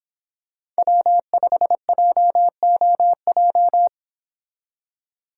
Morse code, heard as a keyed tone, about 25 wpm, 700 Hz. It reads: W5JOJ